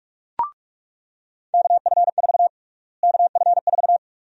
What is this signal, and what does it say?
Morse code, heard as a keyed tone, about 42 wpm, 700 Hz. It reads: KF4 KF4
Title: USA Call Sign Prefixes - Repeated 2x - 45wpm